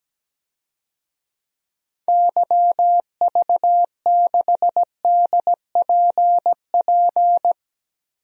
Morse code, heard as a keyed tone, about 17 wpm, 700 Hz. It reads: YV6DPP
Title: DXCC-100 Call Signs - 6 Characters - Repeated 2x - 17wpm